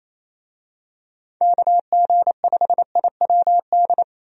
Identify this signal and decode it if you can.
Morse code, heard as a keyed tone, about 27 wpm, 700 Hz. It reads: KG5IWD
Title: USA Call Signs - 6 Characters - Repeated 2x - 28wpm